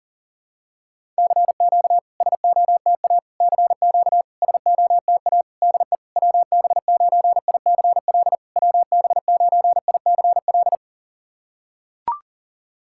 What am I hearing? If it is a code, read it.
Morse code, heard as a keyed tone, about 40 wpm, 700 Hz. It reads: CQ SOTA CQ SOTA DE WB9ICL WB9ICL
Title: SOTA QSO Practice 40wpm